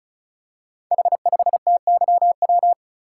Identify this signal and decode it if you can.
Morse code, heard as a keyed tone, about 36 wpm, 690 Hz. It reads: H5TYW
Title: Code Groups - 5 Letter:Numbers - 35wpm